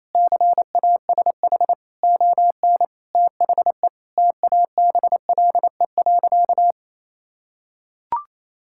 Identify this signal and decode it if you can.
Morse code, heard as a keyed tone, about 28 wpm, 700 Hz. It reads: CASH ON THE TABLE.
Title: Sentences from Top 1000 Words - Review Part 5 - 28wpm